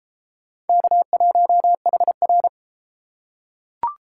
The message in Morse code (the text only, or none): K1HR